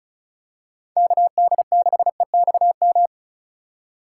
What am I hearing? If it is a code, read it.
Morse code, heard as a keyed tone, about 36 wpm, 700 Hz. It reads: KD6EXM
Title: USA Call Signs - 6 Characters - 3x Character Spacing - 35wpm